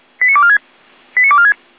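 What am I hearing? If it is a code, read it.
Five-tone selective call (ZVEI), heard as quick runs of five tones, tones of about 70 ms. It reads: 89246 89246
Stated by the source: ZVEI five-tone selective call 89168, received by SDR.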